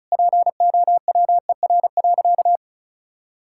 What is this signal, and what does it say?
Morse code, heard as a keyed tone, about 36 wpm, 700 Hz. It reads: POWER.
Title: Sentences from Top 1000 Words - Review Part 6 - 35wpm